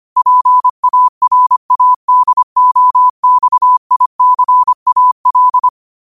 Morse code, heard as a keyed tone, about 25 wpm, 1000 Hz. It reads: PARADOXICAL